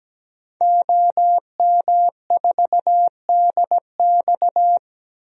Morse code, heard as a keyed tone, about 17 wpm, 700 Hz. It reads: OM4DX
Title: DXCC-100 Call Signs - 5 Characters - 17wpm